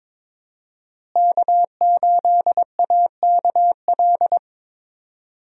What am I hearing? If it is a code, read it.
Morse code, heard as a keyed tone, about 22 wpm, 700 Hz. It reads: K8AKL
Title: USA Call Signs - 5 Characters - Repeated 2x - 22wpm